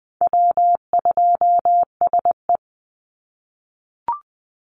Morse code, heard as a keyed tone, about 20 wpm, 700 Hz. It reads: W2SE
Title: NA SOTA Chaser Callsigns 20wpm